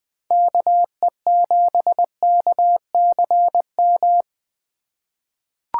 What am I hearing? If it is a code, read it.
Morse code, heard as a keyed tone, about 20 wpm, 700 Hz. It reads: KE7KCM
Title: USA Call Signs - 6 Characters - 3x Character Spacing - 20wpm